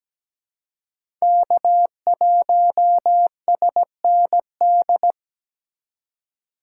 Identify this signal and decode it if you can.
Morse code, heard as a keyed tone, about 17 wpm, 700 Hz. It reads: K1SND
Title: K1USN SST QSO Practice - 17wpm